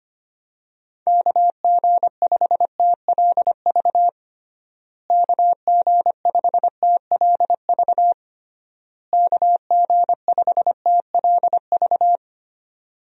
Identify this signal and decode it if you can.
Morse code, heard as a keyed tone, about 25 wpm, 700 Hz. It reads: KG5TLV KG5TLV KG5TLV
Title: USA Call Signs - 6 Characters - Repeated 3x - 25wpm